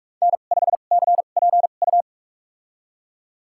Morse code, heard as a keyed tone, about 46 wpm, spaced as for 32 wpm, 700 Hz. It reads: N5CPU
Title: USA Call Signs - 2x Character Spacing - 45wpm